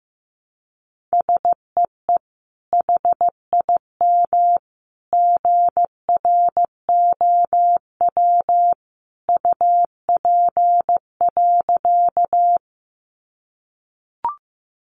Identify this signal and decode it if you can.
Morse code, heard as a keyed tone, about 15 wpm, 700 Hz. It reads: SEE HIM GROW UP.
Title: Sentences from Top 300 Words 15wpm